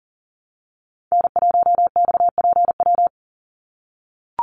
Morse code, heard as a keyed tone, about 40 wpm, 700 Hz. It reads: N1XPW
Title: USA Call Signs 40wpm